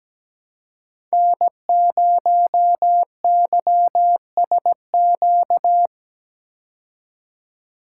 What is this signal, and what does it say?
Morse code, heard as a keyed tone, about 17 wpm, 700 Hz. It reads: N0YSQ
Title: K1USN SST QSO Practice - 17wpm